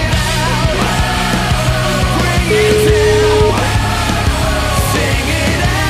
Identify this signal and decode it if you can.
Ringback tone (Europe, 425 Hz), with noise.